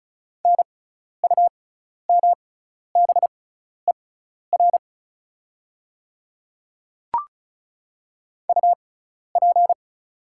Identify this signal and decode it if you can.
Morse code, heard as a keyed tone, about 36 wpm, spaced as for 12 wpm, 700 Hz. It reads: NUMBER UP
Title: 6x Character Spacing - Top 100 Words - 35wpm